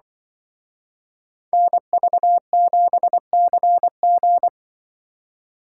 Morse code, heard as a keyed tone, about 23 wpm, 700 Hz. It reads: NV7CG
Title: Speed Racing - USA 2x2 Call Signs 20wpm